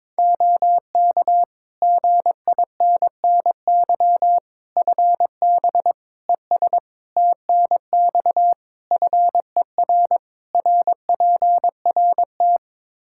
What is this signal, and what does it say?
Morse code, heard as a keyed tone, about 22 wpm, 700 Hz. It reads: OK GINNY FB ES TNX FER RPRT
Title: Practice QSO - Exchange 2 - Info - 22wpm